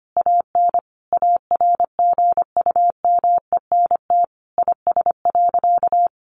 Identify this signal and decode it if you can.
Morse code, heard as a keyed tone, about 25 wpm, 700 Hz. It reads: AN ARGUMENT IS.
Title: Sentences from Top 1400 Words 25wpm